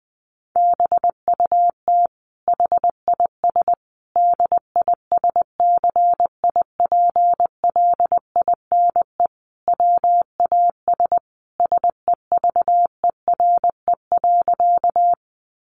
Morse code, heard as a keyed tone, about 20 wpm, 700 Hz. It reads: BUT HIS DISCIPLINE WAS SEVERE.
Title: Sentences from Top 1400 Words 20wpm